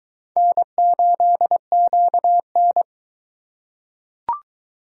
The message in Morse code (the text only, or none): N8QN